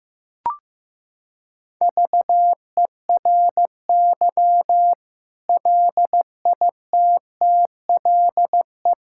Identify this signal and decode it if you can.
Morse code, heard as a keyed tone, about 15 wpm, 700 Hz. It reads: VERY LITTLE
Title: Sentences from Top 1000 Words - Review Part 6 - 15wpm